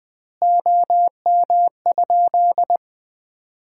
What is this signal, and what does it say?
Morse code, heard as a keyed tone, about 20 wpm, 700 Hz. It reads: OM?
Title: Common QSO Phrases - 2x Word-Spacing - 20wpm